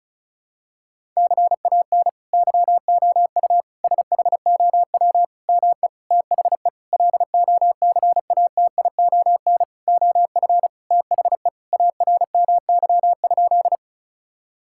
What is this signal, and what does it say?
Morse code, heard as a keyed tone, about 34 wpm, 700 Hz. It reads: CAN YOU SHOW ME THE LOCATION OF THE ARMY?
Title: Sentences from Top 1000 Words - Review Part 6 - 35wpm